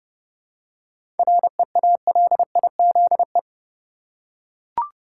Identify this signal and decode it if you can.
Morse code, heard as a keyed tone, about 30 wpm, 700 Hz. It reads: REALIZE
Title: Seven Letter Words - 30wpm